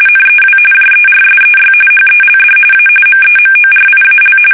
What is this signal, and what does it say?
RTTY, about 45.45 baud. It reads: NEW JERSEY HELD A MEMORIAL
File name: fsk850 / battleshipnj